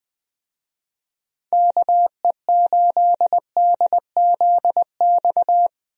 Morse code, heard as a keyed tone, about 21 wpm, 700 Hz. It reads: KE8DZX
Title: USA Call Signs - 6 Characters - Repeated 3x - 20wpm